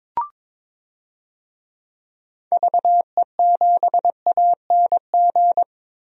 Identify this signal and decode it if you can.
Morse code, heard as a keyed tone, about 21 wpm, 700 Hz. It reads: VE7ANG
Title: DXCC-100 Call Signs - 6 Characters - Repeated 3x - 22wpm